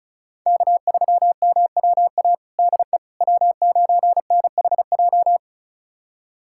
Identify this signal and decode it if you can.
Morse code, heard as a keyed tone, about 34 wpm, 700 Hz. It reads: K3MWA DE W9NHJ
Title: USA Callsign DE Callsign - 5 Characters - 35wpm